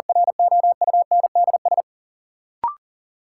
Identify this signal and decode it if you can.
Morse code, heard as a keyed tone, about 40 wpm, 700 Hz. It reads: ROUNDS
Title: Sentences from Top 1300 Words - New Word Drill - 40wpm